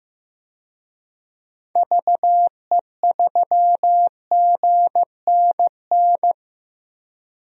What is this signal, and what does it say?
Morse code, heard as a keyed tone, about 15 wpm, 700 Hz. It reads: VE3GNN
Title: DXCC-100 Call Signs - 6 Characters - 15wpm